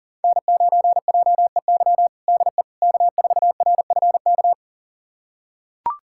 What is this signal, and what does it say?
Morse code, heard as a keyed tone, about 40 wpm, 700 Hz. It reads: N9JEY DE K4RFK
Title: USA Callsign DE Callsign - 5 Characters - 40wpm